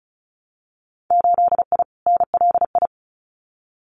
Morse code, heard as a keyed tone, about 36 wpm, 700 Hz. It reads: 8I NLI
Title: Winter Field Day Exchanges Encore 35wpm